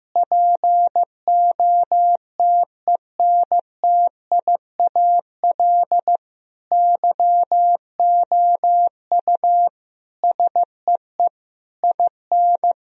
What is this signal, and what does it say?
Morse code, heard as a keyed tone, about 15 wpm, 700 Hz. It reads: POTENTIAL YOU SEE IN
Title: Sentences from Top 1000 Words - Review Part 4 - 15wpm